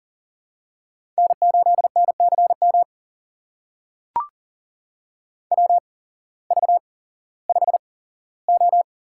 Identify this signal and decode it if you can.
Morse code, heard as a keyed tone, about 40 wpm, spaced as for 19 wpm, 700 Hz. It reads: N8NCM WV5O
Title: USA Call Signs - 8x Character Spacing - 40wpm